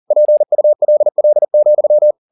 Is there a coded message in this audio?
Yes: Morse code from 0.1 to 2.1 s, heard as a keyed tone, about 40 wpm, 600 Hz. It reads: PULL,